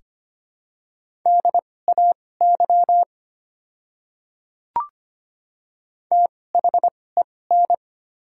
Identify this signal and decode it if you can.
Morse code, heard as a keyed tone, about 25 wpm, spaced as for 18 wpm, 690 Hz. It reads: DAY THEN
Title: 2x Character Spacing - Top 100 Words - 25wpm